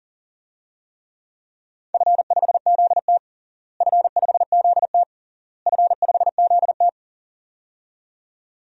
Morse code, heard as a keyed tone, about 40 wpm, 700 Hz. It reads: F5ZT F5ZT F5ZT
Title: DXCC-100 Call Signs - 4 Characters - Repeated 3x - 40wpm